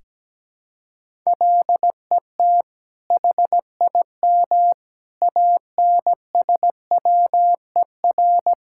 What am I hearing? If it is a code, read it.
Morse code, heard as a keyed tone, about 17 wpm, 700 Hz. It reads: LET HIM ANSWER
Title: Sentences from Top 1000 Words - Review Part 5 - 17wpm